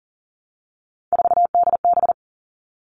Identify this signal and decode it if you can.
Morse code, heard as a keyed tone, about 40 wpm, 700 Hz. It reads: 4DB
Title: All Cut Numbers - 3-Digit 40wpm